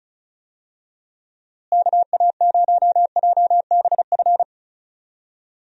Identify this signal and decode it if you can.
Morse code, heard as a keyed tone, about 36 wpm, 700 Hz. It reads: KA0JBF
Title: USA Call Signs - 6 Characters - Repeated 3x - 35wpm